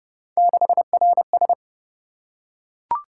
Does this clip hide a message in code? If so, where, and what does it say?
Yes: Morse code from 0.4 to 1.5 s, heard as a keyed tone, about 28 wpm, 700 Hz. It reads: 6RS